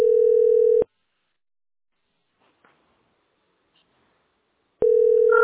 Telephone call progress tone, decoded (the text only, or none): ringback tone